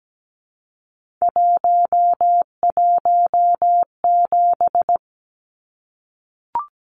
Morse code, heard as a keyed tone, about 17 wpm, 700 Hz. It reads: 117
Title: Numbers - 3-Digit 17wpm